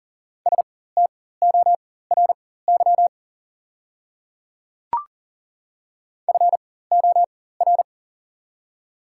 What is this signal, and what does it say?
Morse code, heard as a keyed tone, about 40 wpm, spaced as for 19 wpm, 700 Hz. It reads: STORY FOR
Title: 4x Character Spacing - Top 300 Words - 40wpm